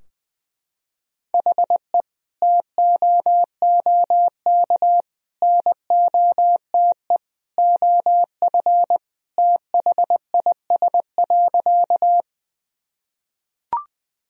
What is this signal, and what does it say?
Morse code, heard as a keyed tone, about 20 wpm, 700 Hz. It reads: HE TOOK NOTE OF THIS.
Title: Sentences from Top 1000 Words - Review Part 2 - 20wpm